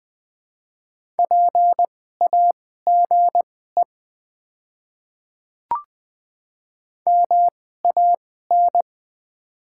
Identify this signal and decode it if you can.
Morse code, heard as a keyed tone, about 20 wpm, spaced as for 14 wpm, 700 Hz. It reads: PAGE MAN
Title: 2x Character Spacing - Top 300 Words - 20wpm